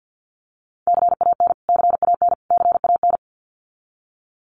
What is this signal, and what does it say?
Morse code, heard as a keyed tone, about 50 wpm, 700 Hz. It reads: CAN CAN CAN
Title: Top 200 Words Repeated 50wpm